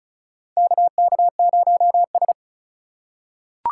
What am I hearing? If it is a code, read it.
Morse code, heard as a keyed tone, about 36 wpm, 700 Hz. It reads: KK0S
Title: USA Call Signs - 4 Characters - 4x Character Spacing - 35wpm